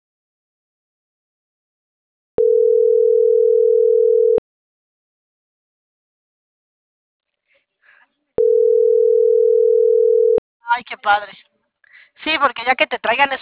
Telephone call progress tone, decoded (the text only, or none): ringback tone